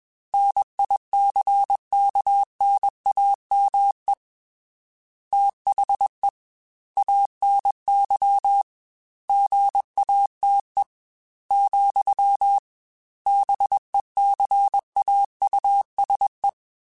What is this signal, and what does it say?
Morse code, heard as a keyed tone, about 21 wpm, 800 Hz. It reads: NICKNAME THE ANY GATE, BECAUSE
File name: Technical Subjects / Technical Subjects-mp3s / OR Gate @20